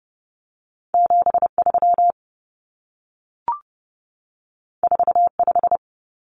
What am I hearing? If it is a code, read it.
Morse code, heard as a keyed tone, about 30 wpm, 700 Hz. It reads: 73 45
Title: Numbers - 2-Digit 30wpm